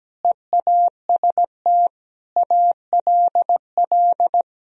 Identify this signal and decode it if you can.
Morse code, heard as a keyed tone, about 17 wpm, 700 Hz. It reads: EAST ALL
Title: Sentences from Top 1000 Words - Review Part 3 - 17wpm